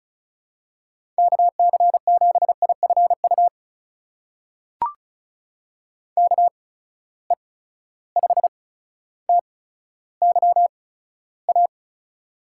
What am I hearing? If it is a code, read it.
Morse code, heard as a keyed tone, about 36 wpm, spaced as for 15 wpm, 700 Hz. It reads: KC7IFU KE5TYA
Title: USA Call Signs - 6 Characters - 8x Character Spacing - 35wpm